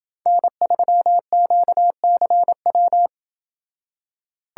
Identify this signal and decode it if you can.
Morse code, heard as a keyed tone, about 26 wpm, 700 Hz. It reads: N3QCW